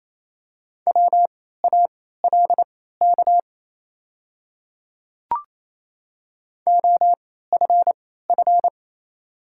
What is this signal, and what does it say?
Morse code, heard as a keyed tone, about 27 wpm, spaced as for 16 wpm, 700 Hz. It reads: WALK OFF
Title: 3x Character Spacing - Top 300 Words - 28wpm